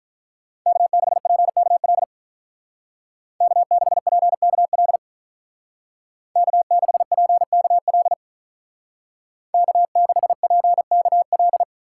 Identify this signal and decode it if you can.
Morse code, heard as a keyed tone, about 44 wpm, 700 Hz. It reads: K6PKL K6PKL K6PKL K6PKL